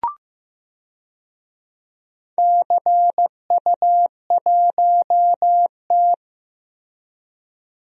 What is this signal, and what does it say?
Morse code, heard as a keyed tone, about 15 wpm, 700 Hz. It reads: CU1T
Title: DXCC-100 Call Signs - 4 Characters - Repeated 3x - 15wpm